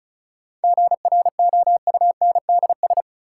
Morse code, heard as a keyed tone, about 34 wpm, 700 Hz. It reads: GROUNDS
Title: Sentences from Top 1300 Words - New Word Drill Alt - 35wpm